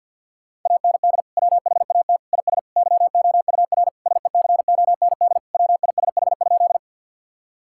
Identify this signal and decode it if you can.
Morse code, heard as a keyed tone, about 50 wpm, 700 Hz. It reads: AND WHAT IS YOUR SECOND WISH?
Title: Sentences from Top 1600 Words 50wpm